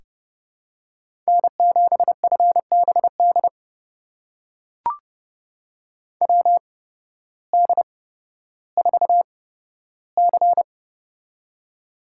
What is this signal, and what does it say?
Morse code, heard as a keyed tone, about 30 wpm, spaced as for 14 wpm, 700 Hz. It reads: N7FBD WD4C
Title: USA Call Signs - 8x Character Spacing - 30wpm